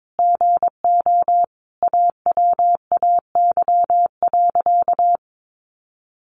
Morse code, heard as a keyed tone, about 22 wpm, 700 Hz. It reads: GO AWAY.
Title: Sentences from Top 1500 Words 22wpm